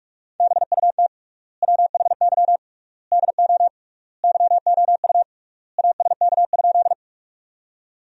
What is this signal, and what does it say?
Morse code, heard as a keyed tone, about 44 wpm, 700 Hz. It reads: BUT WHY DO YOU ASK?
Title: 3x Word-Spacing - Sentences from Top 200 Words - 45wpm